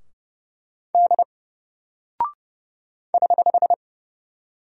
Morse code, heard as a keyed tone, about 30 wpm, 700 Hz. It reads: D <HH>
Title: Single Character (Letter:Number:Punctuation:Prosign) Alt 30wpm